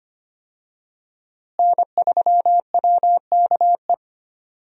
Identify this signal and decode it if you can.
Morse code, heard as a keyed tone, about 25 wpm, 700 Hz. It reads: N3WKE
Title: USA Call Signs - 5 Characters - Repeated 2x - 25wpm